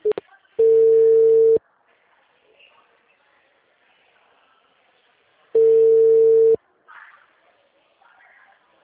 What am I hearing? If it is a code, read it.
Ringback tone (China, 450 Hz).